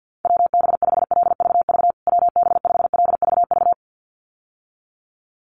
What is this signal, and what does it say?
Morse code, heard as a keyed tone, about 50 wpm, 700 Hz. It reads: WB5LVV WB5LVV
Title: USA Call Signs Repeated 2x 50wpm